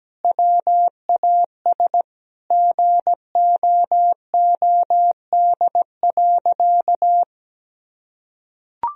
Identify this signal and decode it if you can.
Morse code, heard as a keyed tone, about 17 wpm, 700 Hz. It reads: WAS GOOD.
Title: Sentences from Top 1000 Words - Review Part 2 - 17wpm